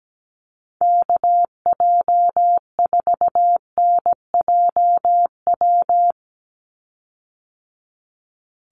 Morse code, heard as a keyed tone, about 17 wpm, 700 Hz. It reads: KJ4NJW
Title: USA Call Signs 17wpm